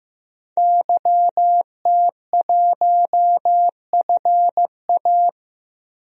Morse code, heard as a keyed tone, about 15 wpm, 700 Hz. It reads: YT1FA
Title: DXCC-100 Call Signs - 5 Characters - 15wpm